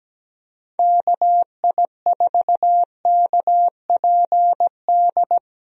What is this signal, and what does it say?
Morse code, heard as a keyed tone, about 17 wpm, 700 Hz. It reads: KI4KPD
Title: USA Call Signs - 6 Characters - 17wpm